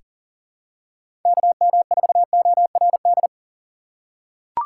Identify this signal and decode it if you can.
Morse code, heard as a keyed tone, about 40 wpm, 700 Hz. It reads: KM4ORD
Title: USA Call Signs - 6 Characters - 2x Character Spacing - 40wpm